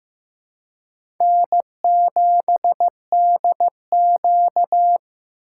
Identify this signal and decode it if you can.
Morse code, heard as a keyed tone, about 15 wpm, 700 Hz. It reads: N7DQ
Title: USA Call Signs - 4 Characters - 3x Character Spacing - 15wpm